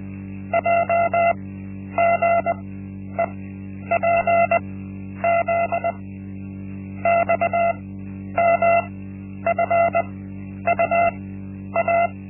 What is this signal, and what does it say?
Morse code, heard as a keyed tone, about 20 wpm, spaced as for 10 wpm, 680 Hz. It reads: JGEPZ XMFUA
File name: Linksmakalnis-1